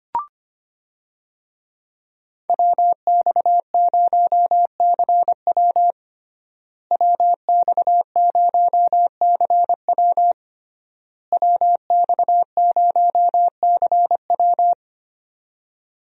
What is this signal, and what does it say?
Morse code, heard as a keyed tone, about 25 wpm, 700 Hz. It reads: WX0CW WX0CW WX0CW
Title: USA Call Signs - 5 Characters - Repeated 3x - 25wpm